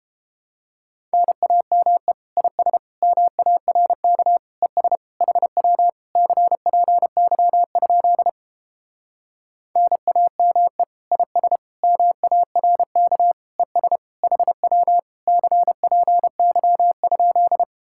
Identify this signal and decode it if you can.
Morse code, heard as a keyed tone, about 29 wpm, 700 Hz. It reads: NAME IS MARK ES HW CPY? NAME IS MARK ES HW CPY?